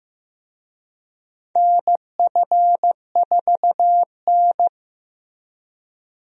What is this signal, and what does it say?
Morse code, heard as a keyed tone, about 15 wpm, 700 Hz. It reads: NF4N
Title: USA Call Signs - 4 Characters - Repeated 2x - 15wpm